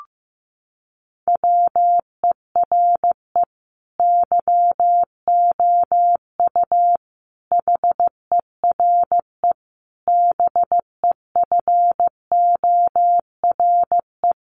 Morse code, heard as a keyed tone, about 15 wpm, 700 Hz. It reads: WERE YOU HERE BEFORE
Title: Sentences from Top 200 Words 15wpm